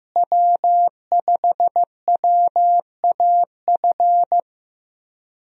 Morse code, heard as a keyed tone, about 15 wpm, 700 Hz. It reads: W5WAF
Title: USA Call Signs - 5 Characters - Repeated 3x - 15wpm